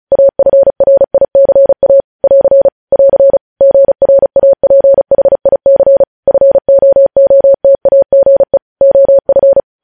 Morse code, heard as a keyed tone, about 36 wpm, 550 Hz. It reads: AFRICA <AR> <AR> GRAPHIC FOOTAGE OF
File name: podcast / News headlines-1x-35x35